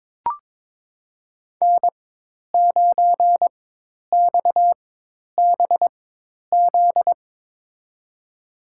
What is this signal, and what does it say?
Morse code, heard as a keyed tone, about 22 wpm, spaced as for 10 wpm, 700 Hz. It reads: N9XBZ
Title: USA Call Signs - 5 Characters - 4x Character Spacing - 22wpm